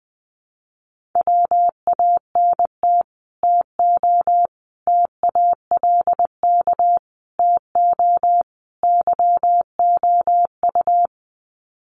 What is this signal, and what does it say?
Morse code, heard as a keyed tone, about 20 wpm, 700 Hz. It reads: WANT TO TALK TO YOU
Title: Sets of 5 Words Encore 20wpm